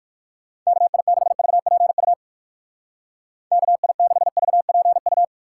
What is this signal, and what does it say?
Morse code, heard as a keyed tone, about 59 wpm, 700 Hz. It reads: KI6VPU KI6VPU